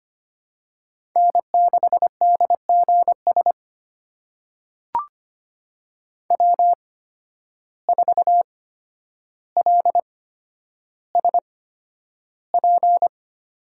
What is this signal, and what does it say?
Morse code, heard as a keyed tone, about 25 wpm, spaced as for 11 wpm, 700 Hz. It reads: N6DGS W4LSP
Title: USA Call Signs - 5 Characters - 8x Character Spacing - 25wpm